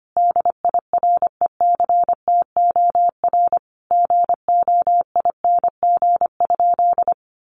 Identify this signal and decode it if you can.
Morse code, heard as a keyed tone, about 25 wpm, 700 Hz. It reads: DIRECTOR GOING?
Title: Sentences from Top 1300 Words 25wpm